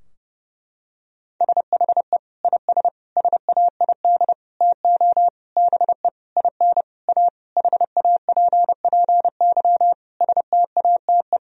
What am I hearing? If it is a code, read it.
Morse code, heard as a keyed tone, about 30 wpm, 700 Hz. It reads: SHE IS SAID TO BE IN A HAPPY STATE
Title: Sentences from Top 1000 Words - Review Part 2 - 30wpm